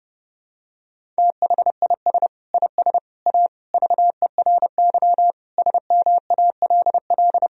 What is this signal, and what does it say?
Morse code, heard as a keyed tone, about 30 wpm, 700 Hz. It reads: THIS IS A VERY SMALL
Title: Sentences from Top 1000 Words - Review Part 1 - 30wpm